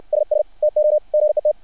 Morse code, heard as a keyed tone, about 60 wpm, 600 Hz. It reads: SS TO GET
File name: archived / CRSnet